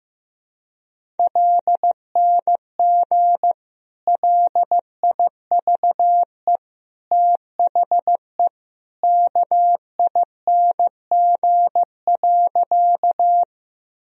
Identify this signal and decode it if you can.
Morse code, heard as a keyed tone, about 15 wpm, 700 Hz. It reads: LNG LIVE THE KING.
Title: Sentences from Top 400 Words and Abbreviations 15wpm